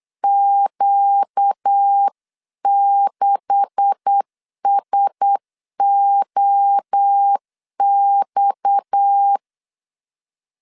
Morse code, heard as a keyed tone, about 8 wpm, 800 Hz. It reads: Q6SOX
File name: CW / DET / 1 9 1500C S C 8wpm 5653